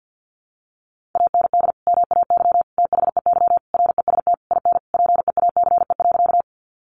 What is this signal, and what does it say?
Morse code, heard as a keyed tone, about 50 wpm, 700 Hz. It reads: AND MAY THEY REST IN PEACE.
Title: Sentences from Top 1500 Words 50wpm